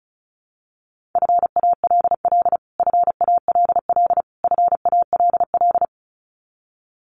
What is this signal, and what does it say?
Morse code, heard as a keyed tone, about 34 wpm, 700 Hz. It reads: FALL FALL FALL
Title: Top 500 Words Repeated 35wpm